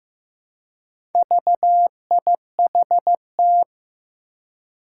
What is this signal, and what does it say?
Morse code, heard as a keyed tone, about 15 wpm, 700 Hz. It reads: VIHT
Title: Four Challenging Letter-Number Combinations 15wpm